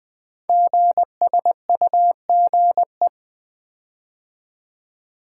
Morse code, heard as a keyed tone, about 20 wpm, 700 Hz. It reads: GSUGE
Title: Code Groups - 5 Letters - 20wpm